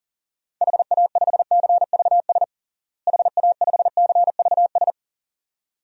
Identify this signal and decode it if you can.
Morse code, heard as a keyed tone, about 37 wpm, 700 Hz. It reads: HA5CVS HA5CVS
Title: DXCC-100 Call Signs - 6 Characters - Repeated 2x - 40wpm